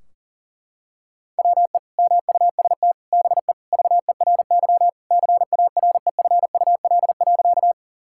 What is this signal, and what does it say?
Morse code, heard as a keyed tone, about 40 wpm, 700 Hz. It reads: WE MUST BE VERY CAREFUL.
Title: Sentences from Top 1600 Words 40wpm